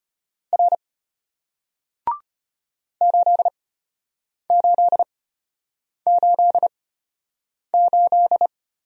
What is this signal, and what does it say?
Morse code, heard as a keyed tone, about 33 wpm, spaced as for 8 wpm, 700 Hz. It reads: R 8888